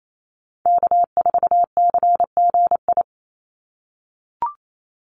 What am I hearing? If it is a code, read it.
Morse code, heard as a keyed tone, about 28 wpm, 700 Hz. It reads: K4CGI
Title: USA Call Signs 28wpm